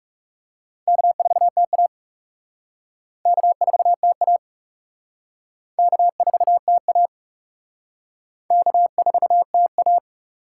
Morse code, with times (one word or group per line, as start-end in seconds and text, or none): 0.9-1.9 K4TA
3.3-4.4 K4TA
5.8-7.1 K4TA
8.5-10.0 K4TA